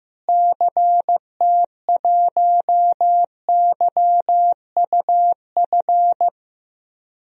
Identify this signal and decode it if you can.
Morse code, heard as a keyed tone, about 15 wpm, 700 Hz. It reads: CT1YUF
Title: DXCC-100 Call Signs - 6 Characters - Repeated 3x - 15wpm